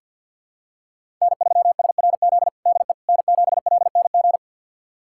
Morse code, heard as a keyed tone, about 53 wpm, 700 Hz. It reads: N3SRZ DE N7LNG
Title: USA Callsign DE Callsign - 5 Characters - 50wpm